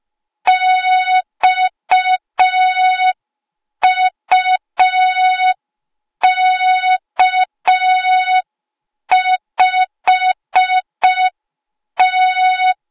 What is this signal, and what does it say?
Morse code, heard as a keyed tone, about 5 wpm, 3700 Hz. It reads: XUK5T